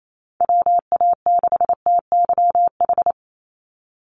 Morse code, heard as a keyed tone, about 28 wpm, 700 Hz. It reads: WA6TYH
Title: USA Call Signs 28wpm